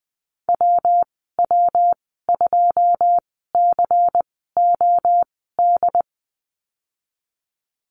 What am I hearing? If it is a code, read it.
Morse code, heard as a keyed tone, about 20 wpm, spaced as for 15 wpm, 700 Hz. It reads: WW2COD
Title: 219 Lesson 35g - Number 0 - Call Signs 20wpm